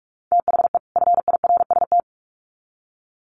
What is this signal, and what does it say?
Morse code, heard as a keyed tone, about 44 wpm, 700 Hz. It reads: THE FIRST
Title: Sets of 2 Words 45wpm